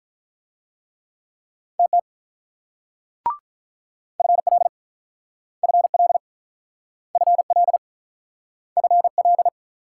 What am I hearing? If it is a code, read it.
Morse code, heard as a keyed tone, about 44 wpm, 700 Hz. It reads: TT FL FL FL FL